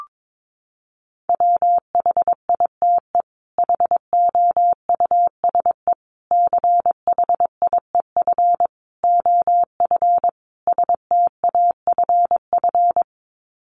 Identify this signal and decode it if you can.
Morse code, heard as a keyed tone, about 22 wpm, 700 Hz. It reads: WHITE HOUSE CHIEF OF STAFF
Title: Sets of 5 Words Encore 22wpm